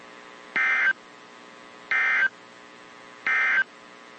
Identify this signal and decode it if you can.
Emergency Alert System (SAME) tones, heard as harsh buzzing bursts. It is heard from 0.4 to 3.7 s.
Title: WZ2523 - NOAA Weather Radio